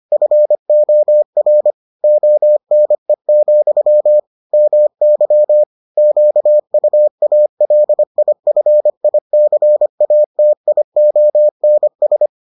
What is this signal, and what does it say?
Morse code, heard as a keyed tone, about 25 wpm, 600 Hz. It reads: FOR ONE, MY QUALIFICATIONS